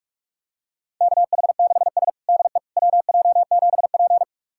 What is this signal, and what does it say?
Morse code, heard as a keyed tone, about 41 wpm, 700 Hz. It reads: KH6S DE WJ7P
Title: USA Callsign DE Callsign - 4 Characters - 45wpm